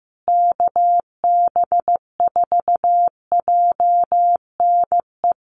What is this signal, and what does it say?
Morse code, heard as a keyed tone, about 15 wpm, 700 Hz. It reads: KB4JNE
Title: USA Call Signs Repeated 3x 15wpm